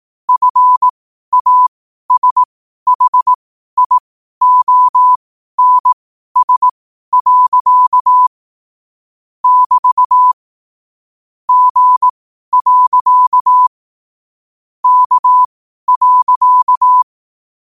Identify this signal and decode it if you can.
Morse code, heard as a keyed tone, about 18 wpm, spaced as for 12 wpm, 1000 Hz. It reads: FASHIONS. <BT> G. K.
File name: qotdfiles / 12wpm